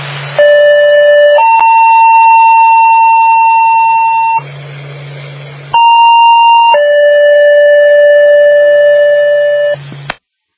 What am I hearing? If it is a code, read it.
Two-tone page, heard as a one-second tone, then a three-second tone, about 590 then 900 Hz, about 930 then 580 Hz. It is heard from 0.4 to 9.7 s.